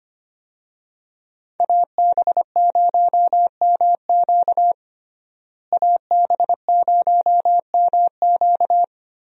Morse code, heard as a keyed tone, about 25 wpm, 700 Hz. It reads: AB0MQ AB0MQ
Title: USA Call Signs - 5 Characters - Repeated 2x - 25wpm